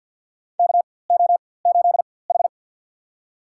Morse code, heard as a keyed tone, about 53 wpm, spaced as for 23 wpm, 700 Hz. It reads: KO8H
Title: USA Call Signs - 4x Character Spacing - 50wpm